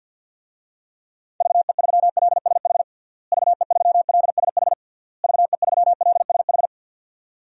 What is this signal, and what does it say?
Morse code, heard as a keyed tone, about 51 wpm, 700 Hz. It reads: VE3LSH VE3LSH VE3LSH
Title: DXCC-100 Call Signs - 6 Characters - Repeated 3x - 50wpm